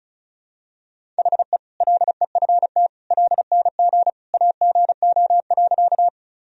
Morse code, heard as a keyed tone, about 34 wpm, 700 Hz. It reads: HE LEFT LNG AGO.
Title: Sentences from Top 500 Words and Abbreviations 35wpm